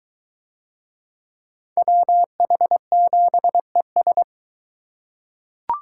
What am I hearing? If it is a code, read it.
Morse code, heard as a keyed tone, about 20 wpm, 700 Hz. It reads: WH7ES